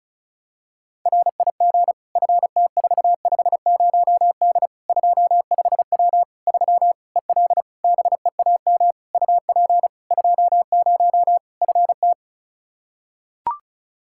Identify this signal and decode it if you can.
Morse code, heard as a keyed tone, about 36 wpm, 700 Hz. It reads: RIG FT450D 25W 3 EL BEAM UP 20 FT
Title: Practice QSO - Exchange 2 - Info - 35wpm